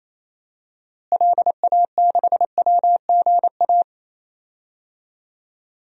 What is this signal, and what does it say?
Morse code, heard as a keyed tone, about 28 wpm, 700 Hz. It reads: LA6WGA
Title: DXCC-100 Call Signs - 6 Characters - Repeated 2x - 28wpm